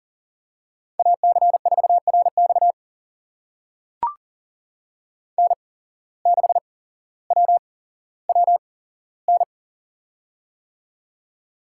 Morse code, heard as a keyed tone, about 40 wpm, spaced as for 17 wpm, 700 Hz. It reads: AC4RX N6WWN
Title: USA Call Signs - 5 Characters - 8x Character Spacing - 40wpm